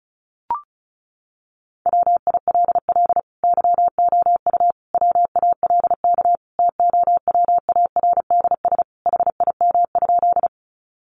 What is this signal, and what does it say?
Morse code, heard as a keyed tone, about 34 wpm, 700 Hz. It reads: WILL YOU WALK TOWARDS HIM?
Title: Sentences from Top 1100 Words 35wpm